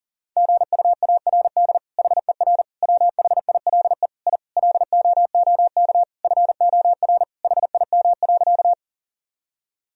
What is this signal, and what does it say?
Morse code, heard as a keyed tone, about 40 wpm, 700 Hz. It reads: GUARD HER WHILE I LOOK FOR HIM.
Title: Sentences from Top 1000 Words - Review Part 4 - 40wpm